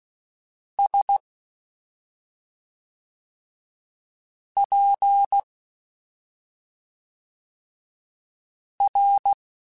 Morse code, heard as a keyed tone, about 16 wpm, 800 Hz. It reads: S P R
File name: Special INGs / Special INGs-mp3s / SlowToFaster ing @5